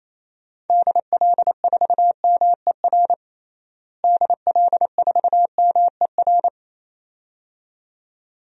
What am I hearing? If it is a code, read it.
Morse code, heard as a keyed tone, about 28 wpm, 700 Hz. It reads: DL4MER DL4MER
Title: DXCC-100 Call Signs - 6 Characters - Repeated 2x - 28wpm